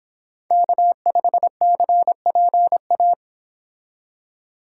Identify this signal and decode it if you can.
Morse code, heard as a keyed tone, about 26 wpm, 700 Hz. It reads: K5CPA